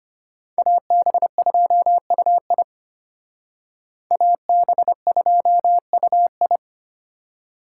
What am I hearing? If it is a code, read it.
Morse code, heard as a keyed tone, about 31 wpm, 700 Hz. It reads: AB2UI AB2UI
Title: Speed Racing - USA Call Signs 25wpm